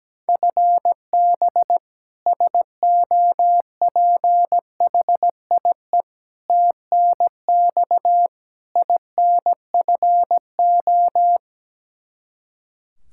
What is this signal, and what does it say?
Morse code, heard as a keyed tone, about 17 wpm, 700 Hz. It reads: FB SOPHIE TNX INFO
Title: K1USN SST QSO Practice 17wpm